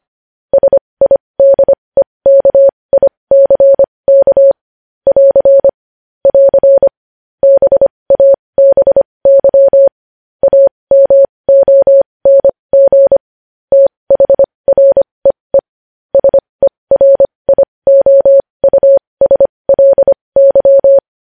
Morse code, heard as a keyed tone, about 25 wpm, spaced as for 20 wpm, 550 Hz. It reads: SIDEKICK <AR> <AR> BABY AMONG THREE SERIOUSLY
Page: news-headlines-in-morse-code-at-20-wpm